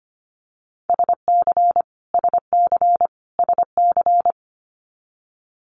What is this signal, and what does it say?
Morse code, heard as a keyed tone, about 25 wpm, 700 Hz. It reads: SC SC SC
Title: Multi-letter Phonemes Repeated 3x 25wpm